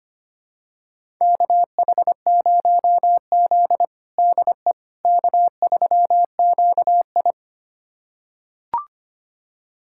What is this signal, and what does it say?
Morse code, heard as a keyed tone, about 25 wpm, 700 Hz. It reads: KH0Z DE K3QI
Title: USA Callsign DE Callsign - 4 Characters - 25wpm